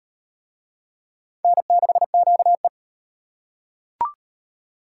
Morse code, heard as a keyed tone, about 38 wpm, 700 Hz. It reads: N6QE